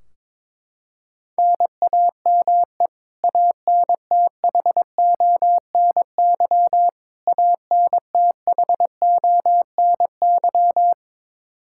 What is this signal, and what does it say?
Morse code, heard as a keyed tone, about 22 wpm, 700 Hz. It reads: NAME ANTHONY ANTHONY
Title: Practice QSO - Exchange 1 - Beginning - 22wpm